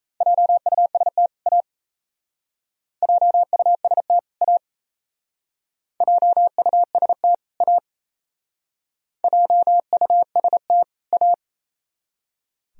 Morse code, heard as a keyed tone, about 35 wpm, 700 Hz. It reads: JUST A JUST A JUST A JUST A
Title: Speed-Racing - Sets of 2 Words 28wpm